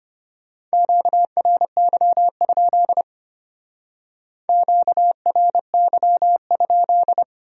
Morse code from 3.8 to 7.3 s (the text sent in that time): QRY?